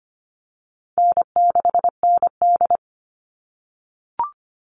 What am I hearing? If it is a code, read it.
Morse code, heard as a keyed tone, about 25 wpm, 700 Hz. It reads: N6ND
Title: All Cut Numbers - 4-Digit 25wpm